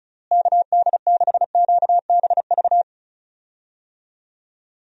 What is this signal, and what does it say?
Morse code, heard as a keyed tone, about 34 wpm, 700 Hz. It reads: KD6QBV
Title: USA Call Signs - 6 Characters - Repeated 2x - 35wpm